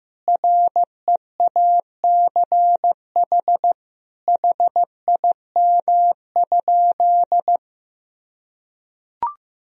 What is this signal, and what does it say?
Morse code, heard as a keyed tone, about 15 wpm, 700 Hz. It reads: REACH HIM?
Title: Sentences from Top 1000 Words - Review Part 2 - 15wpm